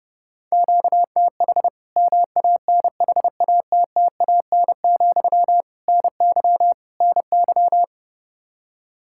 Morse code, heard as a keyed tone, about 30 wpm, 700 Hz. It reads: QTH MANHATTAN, NY NY
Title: Practice QSO - Exchange 1 - Beginning - 30wpm